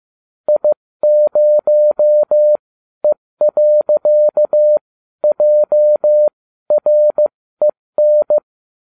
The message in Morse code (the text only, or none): I0E.JREN